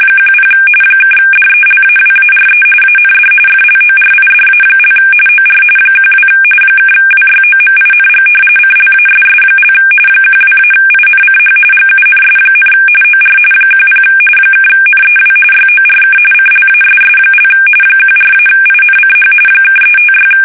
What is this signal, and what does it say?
RTTY, about 45.45 baud. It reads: 10 (DELAWARE AND FLORIDA) OR 12 (WYOMING CLASS) 12-INCH GUNS, OR 10 (NEW YORK CLASS) 14-INCH (356
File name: fsk850 / battleships